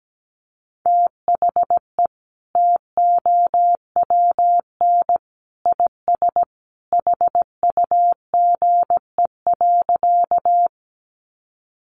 Morse code, heard as a keyed tone, about 17 wpm, 700 Hz. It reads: THE TOWN IS HUGE.
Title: Sentences from Top 1100 Words 17wpm